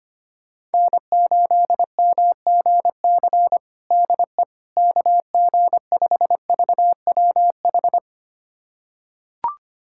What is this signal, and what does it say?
Morse code, heard as a keyed tone, about 25 wpm, 700 Hz. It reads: N8MGC DE KG5VWH
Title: USA Callsign DE Callsign 25wpm